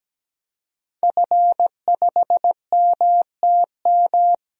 Morse code, heard as a keyed tone, about 17 wpm, 700 Hz. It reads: F5MTM
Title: DXCC-100 Call Signs - 5 Characters - Repeated 3x - 17wpm